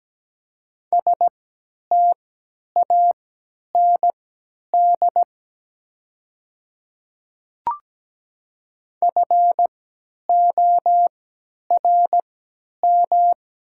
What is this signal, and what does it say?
Morse code, heard as a keyed tone, about 17 wpm, spaced as for 10 wpm, 700 Hz. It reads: STAND FORM
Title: 3x Character Spacing - Top 500 Words - 17wpm